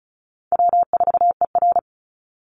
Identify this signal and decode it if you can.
Morse code, heard as a keyed tone, about 34 wpm, 700 Hz. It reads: W4ER
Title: Easy Call Signs Repeated 2x - 35wpm